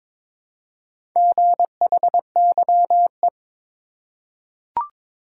Morse code, heard as a keyed tone, about 22 wpm, 700 Hz. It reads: GHYE
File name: practice / 22wpm / Four Letter-Number Combinations - 22wpm